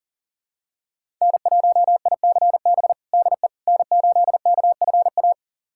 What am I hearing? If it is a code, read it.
Morse code, heard as a keyed tone, about 40 wpm, 700 Hz. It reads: N1ICB DE N8KFA
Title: USA Callsign DE Callsign - 5 Characters - 40wpm